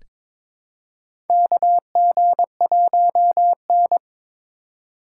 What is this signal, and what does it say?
Morse code, heard as a keyed tone, about 22 wpm, 700 Hz. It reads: KG1N
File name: practice / 22wpm / Four Letter-Number Combinations - 22wpm